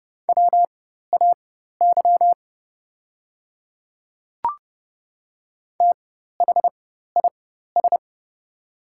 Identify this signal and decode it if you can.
Morse code, heard as a keyed tone, about 30 wpm, spaced as for 14 wpm, 700 Hz. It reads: WAY THIS
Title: 4x Character Spacing - Top 100 Words - 30wpm